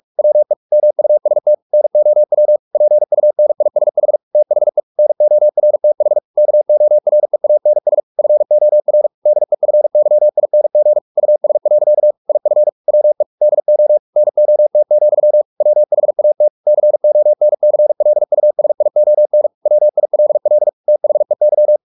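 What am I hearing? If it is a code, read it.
Morse code, heard as a keyed tone, about 44 wpm, 600 Hz. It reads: WE MUST NOW PUNISH THE NORTH KOREANS FOR DEFYING US. IF WE DO NOT, WHAT CONCLUSION WILL THEY
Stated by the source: Morse practice at 45 wpm, the novel Never by Ken Follett in English, from CW med Gnister.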